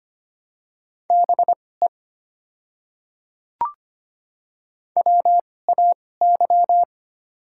Morse code, heard as a keyed tone, about 25 wpm, spaced as for 18 wpm, 690 Hz. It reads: BE WAY
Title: 2x Character Spacing - Top 100 Words - 25wpm